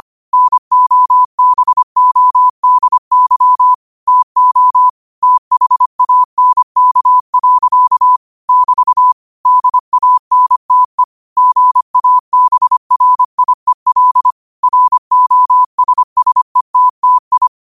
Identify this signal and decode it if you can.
Morse code, heard as a keyed tone, about 25 wpm, 1000 Hz. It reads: NOBODY TO THANK. <BT> DANTE GABRIEL ROSSETTI
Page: Quotes for Fri, 15 Aug 2025 in Morse Code at 25 words per minute.